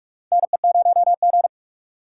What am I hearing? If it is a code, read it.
Morse code, heard as a keyed tone, about 46 wpm, 700 Hz. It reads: NE0G
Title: USA Call Signs - 4 Characters - 4x Character Spacing - 45wpm